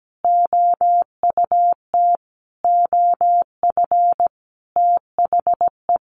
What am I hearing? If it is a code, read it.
Morse code, heard as a keyed tone, about 17 wpm, 700 Hz. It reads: OUT OF THE
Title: Sentences from Top 1500 Words 17wpm